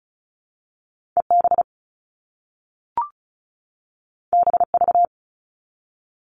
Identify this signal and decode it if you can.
Morse code, heard as a keyed tone, about 36 wpm, 700 Hz. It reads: EB BV
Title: All Cut Numbers - 2-Digit 35wpm